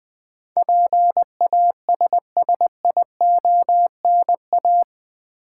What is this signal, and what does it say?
Morse code, heard as a keyed tone, about 20 wpm, 700 Hz. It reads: PASSIONA
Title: Build A Word - 4 Letters At A Time - 10-Letter Words - 20wpm